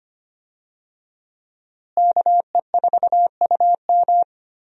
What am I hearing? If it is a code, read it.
Morse code, heard as a keyed tone, about 25 wpm, 700 Hz. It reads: KE4UM
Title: USA Call Signs - 5 Characters - Repeated 2x - 25wpm